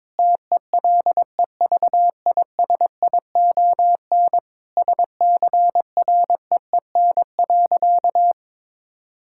Morse code, heard as a keyed tone, about 22 wpm, 700 Hz. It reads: TELEVISION SCREEN.
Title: Sentences from Top 1000 Words - Review Part 6 - 22wpm